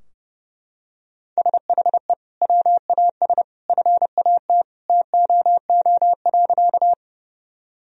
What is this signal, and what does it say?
Morse code, heard as a keyed tone, about 28 wpm, 700 Hz. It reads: SHE WAS FAT TOO.
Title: Sentences from Top 1000 Words - Review Part 4 - 30wpm